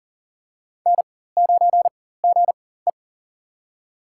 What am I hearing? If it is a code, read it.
Morse code, heard as a keyed tone, about 40 wpm, spaced as for 19 wpm, 700 Hz. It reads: N9GE
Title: USA Call Signs - 4x Character Spacing - 40wpm